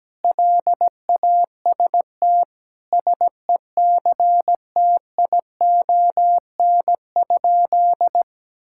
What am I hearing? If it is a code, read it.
Morse code, heard as a keyed tone, about 17 wpm, 700 Hz. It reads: LAST SECTION?
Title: Sentences from Top 1000 Words - Review Part 5 - 17wpm